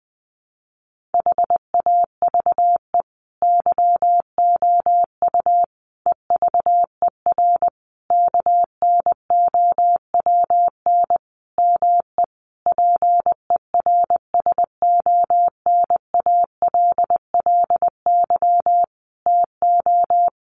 Morse code, heard as a keyed tone, about 19 wpm, 700 Hz. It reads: HAVE YOU EVER KNOWN ME PERSONALLY TO
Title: Sentences from Top 1400 Words 20wpm